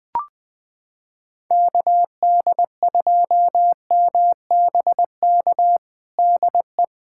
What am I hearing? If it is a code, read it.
Morse code, heard as a keyed tone, about 20 wpm, 700 Hz. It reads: KD2MBK DE
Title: USA Callsign DE Callsign 20wpm